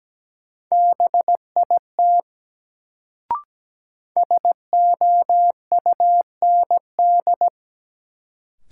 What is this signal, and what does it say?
Morse code, heard as a keyed tone, about 17 wpm, 700 Hz. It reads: BIT SOUND
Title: Sets of 3 Words - New Word Drill - 17wpm